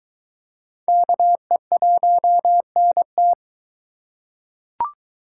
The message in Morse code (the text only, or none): KE1NT